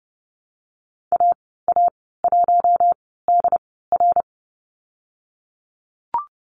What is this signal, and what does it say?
Morse code, heard as a keyed tone, about 30 wpm, spaced as for 17 wpm, 700 Hz. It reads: AA1DR
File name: Lesson 19g - Letter w - Call Signs 30wpm